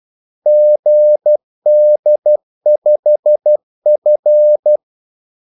Morse code, heard as a keyed tone, about 12 wpm, 600 Hz. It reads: GD5F
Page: 2 ekstra ordmellemrum mellem kaldesignalerne